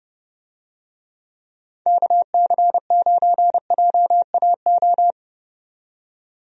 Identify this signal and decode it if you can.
Morse code, heard as a keyed tone, about 32 wpm, 700 Hz. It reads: KC9JAO
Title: USA Call Signs - 6 Characters - 30wpm